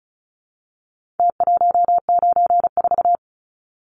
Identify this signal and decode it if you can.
Morse code, heard as a keyed tone, about 34 wpm, 700 Hz. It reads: T194
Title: Numbers - Contest Style - 4-Digit 35wpm